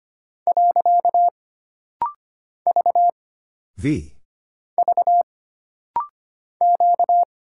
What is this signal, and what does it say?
Morse code, heard as a keyed tone, about 25 wpm, 700 Hz. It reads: . V V Q
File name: Single Character (Letter:Number:Punctuation:Prosign) - Rapid-Fire - 25wpm